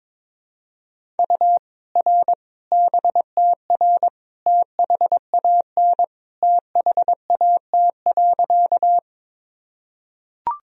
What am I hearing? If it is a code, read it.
Morse code, heard as a keyed tone, about 22 wpm, 700 Hz. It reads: U R BTR THAN THAT.
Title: Sentences from Top 400 Words and Abbreviations 22wpm